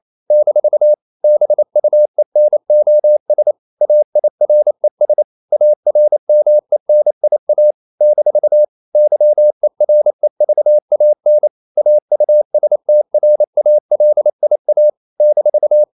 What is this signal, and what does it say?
Morse code, heard as a keyed tone, about 28 wpm, 600 Hz. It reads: - BUENOS AIRES ARMENIA - YEREVAN AUSTRALIA -
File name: CW / Practice / NationalCapitals28